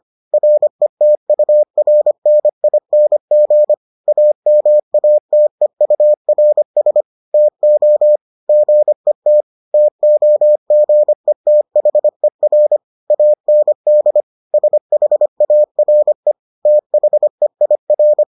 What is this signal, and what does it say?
Morse code, heard as a keyed tone, about 25 wpm, 600 Hz. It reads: RETURNING AMATEURS TO GET TOGETHER AND SHARE THEIR